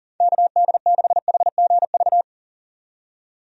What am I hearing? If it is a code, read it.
Morse code, heard as a keyed tone, about 40 wpm, 690 Hz. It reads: KD6HGV
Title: USA Call Signs - 6 Characters - 40wpm